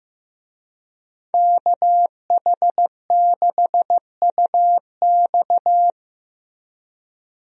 Morse code, heard as a keyed tone, about 16 wpm, 700 Hz. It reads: KH6UX
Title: USA Call Signs - 5 Characters - Repeated 3x - 15wpm